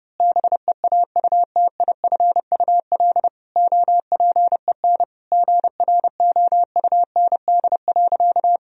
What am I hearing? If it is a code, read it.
Morse code, heard as a keyed tone, about 30 wpm, 700 Hz. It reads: BEAUTIFUL OPEN GROUND.
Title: Sentences from Top 1000 Words - Review Part 2 - 30wpm